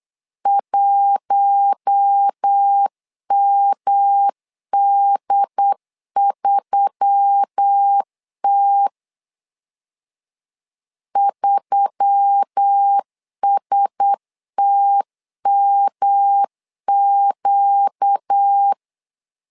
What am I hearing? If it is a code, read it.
Morse code, heard as a keyed tone, about 8 wpm, 800 Hz. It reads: 1MD3T 3STMQ
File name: LEZIONI / 4 Q1S3 8wpm 3037